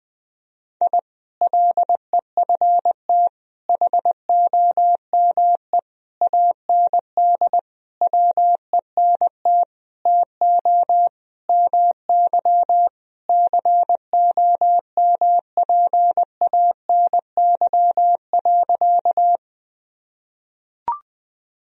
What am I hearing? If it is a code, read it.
Morse code, heard as a keyed tone, about 21 wpm, 700 Hz. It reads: I LEFT HOME AND WENT TO MY COMPANY.
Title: Sentences from Top 1000 Words - Review Part 3 - 20wpm